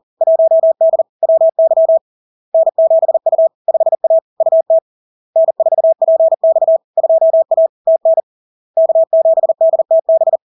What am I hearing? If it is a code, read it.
Morse code, heard as a keyed tone, about 40 wpm, 650 Hz. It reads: 1D WY N7U 5A UT N4PX 2A TN K7DTB
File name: LARK-FD-2022-Log-in-CW-40wpm-650-Hz